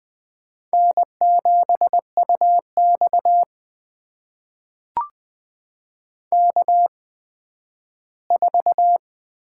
Morse code, heard as a keyed tone, about 20 wpm, spaced as for 12 wpm, 700 Hz. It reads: N7UX K4
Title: USA Call Signs - 4 Characters - 8x Character Spacing - 20wpm